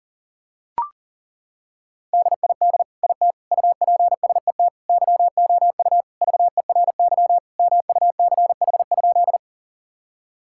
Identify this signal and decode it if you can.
Morse code, heard as a keyed tone, about 40 wpm, 700 Hz. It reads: DID IT UPSET YOU VERY MUCH?
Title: Sentences from Top 1000 Words - Review Part 5 - 40wpm